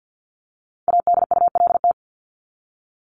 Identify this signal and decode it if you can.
Morse code, heard as a keyed tone, about 48 wpm, 700 Hz. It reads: ADULT
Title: Top 1501-1750 Words 50wpm